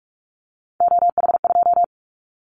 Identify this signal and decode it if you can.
Morse code, heard as a keyed tone, about 44 wpm, 700 Hz. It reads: OH2
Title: Callsign Prefixes for Top 25 DXCC - 45wpm